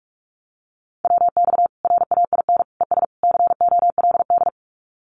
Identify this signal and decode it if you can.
Morse code, heard as a keyed tone, about 44 wpm, 700 Hz. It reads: WX RAIN ES COLD
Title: Common QSO Phrases 45wpm